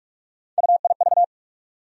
Morse code, heard as a keyed tone, about 46 wpm, 700 Hz. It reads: UIV
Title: USA Call Sign Postfixes - 3 Letters 45wpm